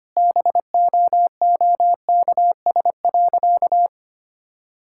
Morse code, heard as a keyed tone, about 25 wpm, 690 Hz. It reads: BOOKS.
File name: Easy Sentences with Hints - 25wpm